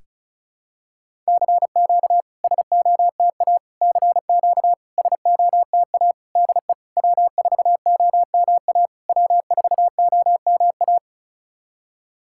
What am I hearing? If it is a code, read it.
Morse code, heard as a keyed tone, about 36 wpm, 690 Hz. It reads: CQ SOTA CQ SOTA DE W4OMA W4OMA
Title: SOTA QSO Practice 35wpm